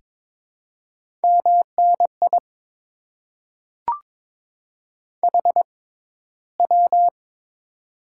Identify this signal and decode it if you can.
Morse code, heard as a keyed tone, about 22 wpm, spaced as for 14 wpm, 700 Hz. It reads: MNI HW
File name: Top 90 QSO Elements - 6x Character Spacing - 22wpm